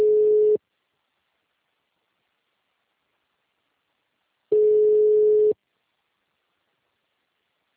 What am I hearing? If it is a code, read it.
Ringback tone (Europe, 425 Hz).